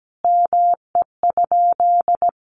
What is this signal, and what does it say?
Morse code, heard as a keyed tone, about 17 wpm, 700 Hz. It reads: ME?
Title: Sentences from Top 200 Words 17wpm